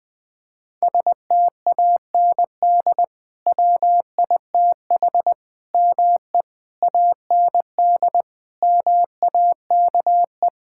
Morse code, heard as a keyed tone, about 21 wpm, 700 Hz. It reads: STAND WITH ME AND MAKE
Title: Sentences from Top 1600 Words 20wpm